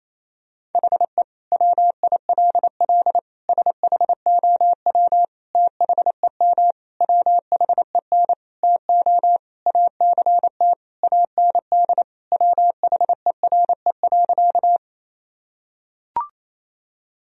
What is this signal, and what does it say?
Morse code, heard as a keyed tone, about 28 wpm, 700 Hz. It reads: HE WILL SHOW THEM WHEN TO ACT AND WHERE.
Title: Sentences from Top 1000 Words - Review Part 1 - 28wpm